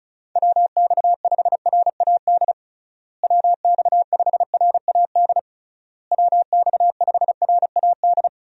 Morse code, heard as a keyed tone, about 34 wpm, 700 Hz. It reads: WX5RAD WX5RAD WX5RAD
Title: USA Call Signs - 6 Characters - Repeated 3x - 35wpm